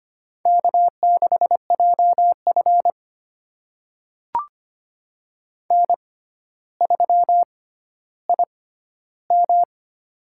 Morse code, heard as a keyed tone, about 25 wpm, spaced as for 13 wpm, 700 Hz. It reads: K6JF N3IM
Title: USA Call Signs - 4 Characters - 6x Character Spacing - 25wpm